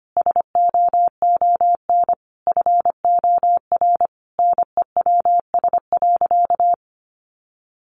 Morse code, heard as a keyed tone, about 25 wpm, 700 Hz. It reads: SOON FOR NEWS.
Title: Sentences from Top 800 Words 25wpm